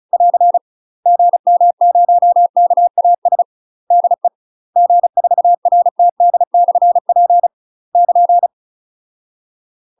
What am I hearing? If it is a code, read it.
Morse code, heard as a keyed tone, about 36 wpm, 700 Hz. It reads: <AR> GM0KAS DE G4RTD/P <KN>
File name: UK MASTER QSOs TRAINING / 35WPM / UK MASTER QSOs 0044